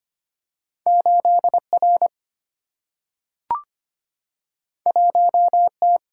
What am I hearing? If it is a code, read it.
Morse code, heard as a keyed tone, about 25 wpm, 700 Hz. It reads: 8R 1T
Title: Single-Letter Single-Number 25wpm